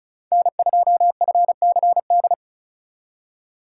Morse code, heard as a keyed tone, about 42 wpm, 700 Hz. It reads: N2FCD
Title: Speed Racing - USA 1x3 Call Signs 35wpm